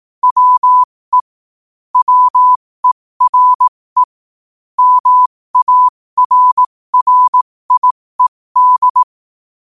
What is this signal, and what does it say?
Morse code, heard as a keyed tone, about 18 wpm, spaced as for 15 wpm, 1000 Hz. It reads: WE WERE MARRIED
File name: qotdfiles / 15wpm